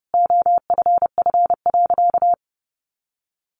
Morse code, heard as a keyed tone, about 30 wpm, 700 Hz. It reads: OFF.
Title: Sentences from Top 1300 Words 30wpm